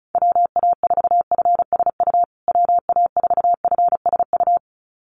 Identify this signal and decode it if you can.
Morse code, heard as a keyed tone, about 34 wpm, 700 Hz. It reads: WA4FSU WA4FSU
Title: USA Call Signs Repeated 2x 35wpm